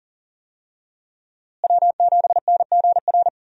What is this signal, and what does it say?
Morse code, heard as a keyed tone, about 40 wpm, 700 Hz. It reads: W7NGR
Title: USA Call Signs - 5 Characters - Repeated 2x - 40wpm